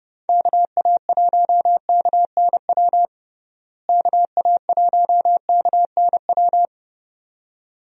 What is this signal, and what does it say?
Morse code, heard as a keyed tone, about 32 wpm, 700 Hz. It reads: KA1KNW KA1KNW
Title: USA Call Signs - 6 Characters - Repeated 2x - 30wpm